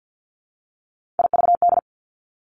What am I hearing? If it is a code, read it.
Morse code, heard as a keyed tone, about 50 wpm, 700 Hz. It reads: IVD
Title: Three Challenging Letter-Number Combinations 50wpm